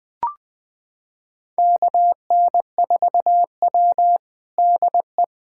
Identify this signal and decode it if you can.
Morse code, heard as a keyed tone, about 20 wpm, 700 Hz. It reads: KN4W DE
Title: USA Callsign DE Callsign - 4 Characters - 20wpm